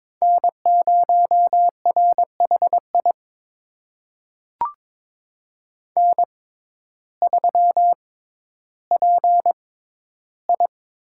Morse code, heard as a keyed tone, about 23 wpm, spaced as for 12 wpm, 700 Hz. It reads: N0RHI N3PI
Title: USA Call Signs - 5 Characters - 6x Character Spacing - 22wpm